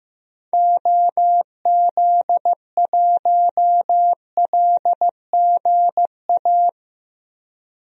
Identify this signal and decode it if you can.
Morse code, heard as a keyed tone, about 15 wpm, 700 Hz. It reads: OZ1LGA
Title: DXCC-100 Call Signs - 6 Characters - Repeated 2x - 15wpm